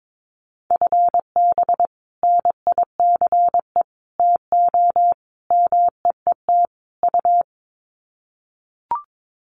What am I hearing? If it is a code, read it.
Morse code, heard as a keyed tone, about 22 wpm, 700 Hz. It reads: FB NICE TO MEET U
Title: Common QSO Phrases 22wpm